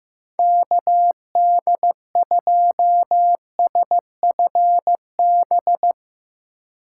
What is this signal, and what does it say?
Morse code, heard as a keyed tone, about 15 wpm, 700 Hz. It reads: KD2SFB
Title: USA Call Signs - 6 Characters - 3x Character Spacing - 15wpm